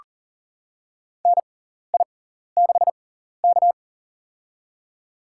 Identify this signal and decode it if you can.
Morse code, heard as a keyed tone, about 37 wpm, spaced as for 14 wpm, 700 Hz. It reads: NI6K
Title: USA Call Signs - 4 Characters - 6x Character Spacing - 40wpm